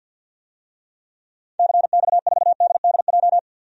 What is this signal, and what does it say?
Morse code, heard as a keyed tone, about 51 wpm, 700 Hz. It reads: CX4DDJ
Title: DXCC-100 Call Signs - 6 Characters - 50wpm